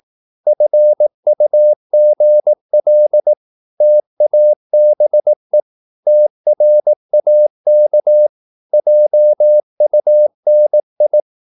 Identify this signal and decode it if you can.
Morse code, heard as a keyed tone, about 18 wpm, 600 Hz. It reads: FUGL TABE TRAK JUNI
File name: Korte ord DK 18wpm